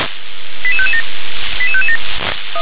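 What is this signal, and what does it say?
Five-tone selective call (ZVEI), heard as quick runs of five tones, tones of about 70 ms. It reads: 88557 88557
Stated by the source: ZVEI five-tone call received on an SDR.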